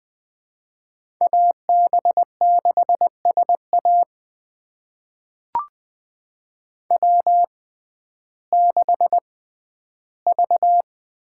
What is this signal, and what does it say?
Morse code, heard as a keyed tone, about 20 wpm, spaced as for 12 wpm, 700 Hz. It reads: AB6SA W6V
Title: USA Call Signs - 6x Character Spacing - 20wpm